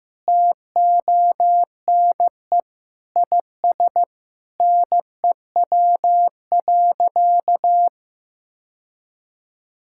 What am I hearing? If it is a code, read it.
Morse code, heard as a keyed tone, about 15 wpm, 700 Hz. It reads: TONE IS NEW.
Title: Sentences from Top 1600 Words 15wpm